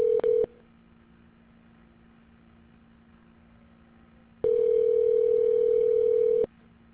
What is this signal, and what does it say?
Ringback tone (North America), 440 + 480 Hz.